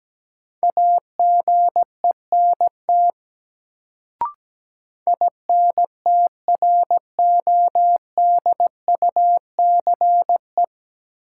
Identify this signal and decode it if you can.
Morse code, heard as a keyed tone, about 17 wpm, 700 Hz. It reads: AGENT INTRODUCE
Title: Sentences from Top 1600 Words - New Word Drill - 17wpm